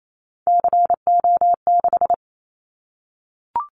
Morse code, heard as a keyed tone, about 28 wpm, 700 Hz. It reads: CO6
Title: Callsign Prefixes for Top 100 DXCC - 28wpm